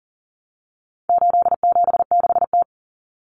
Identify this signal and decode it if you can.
Morse code, heard as a keyed tone, about 40 wpm, 700 Hz. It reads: 876T
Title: Common Cut Numbers - 4-Digit 40wpm